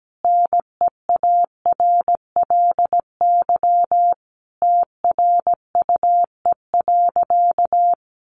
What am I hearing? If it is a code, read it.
Morse code, heard as a keyed tone, about 17 wpm, 700 Hz. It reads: NEARLY TRUE.
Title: Sentences from Top 1300 Words 17wpm